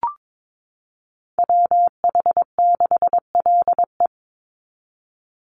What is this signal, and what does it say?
Morse code, heard as a keyed tone, about 22 wpm, 700 Hz. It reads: WH6LE
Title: NA SOTA Chaser Callsigns 22wpm